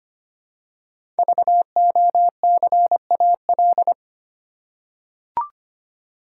Morse code, heard as a keyed tone, about 25 wpm, 700 Hz. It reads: VOCAL
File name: 5 Letter Words - First 3 Letters Given - 25wpm